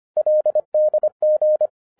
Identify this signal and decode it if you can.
Morse code, heard as a keyed tone, about 24 wpm, 610 Hz. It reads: LDG
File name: 2011-07-mp3 / n24 QRP OTAKU 25wpm